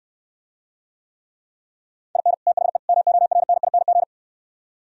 Morse code, heard as a keyed tone, about 67 wpm, 700 Hz. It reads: IN THE MORNING